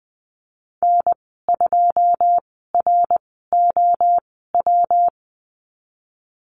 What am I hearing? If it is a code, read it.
Morse code, heard as a keyed tone, about 20 wpm, spaced as for 15 wpm, 700 Hz. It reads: N2ROW
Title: 128 Lesson 21g - Number 6 - Call Signs 20wpm